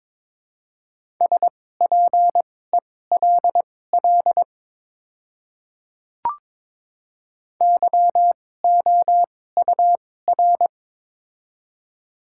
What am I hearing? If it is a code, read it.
Morse code, heard as a keyed tone, about 22 wpm, spaced as for 16 wpm, 700 Hz. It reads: SPELL YOUR
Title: 2x Character Spacing - Top 500 Words - 22wpm